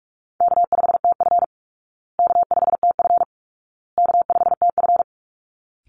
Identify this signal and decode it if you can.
Morse code, heard as a keyed tone, about 44 wpm, 700 Hz. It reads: K5TF K5TF K5TF
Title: Call Signs Repeated 2x - 45wpm